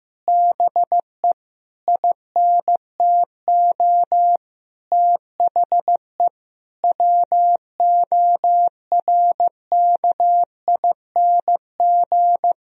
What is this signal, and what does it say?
Morse code, heard as a keyed tone, about 16 wpm, 700 Hz. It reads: BE INTO THE WORKING
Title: Sentences from Top 1000 Words - Review Part 4 - 15wpm